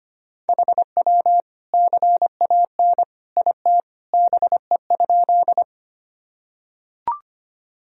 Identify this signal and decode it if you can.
Morse code, heard as a keyed tone, about 25 wpm, 700 Hz. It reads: HW CAN IT BE?
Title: Sentences from Top 100 Words and Abbreviations 25wpm